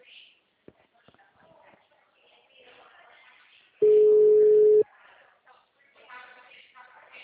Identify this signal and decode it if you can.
Ringback tone (Europe, 425 Hz), with speech.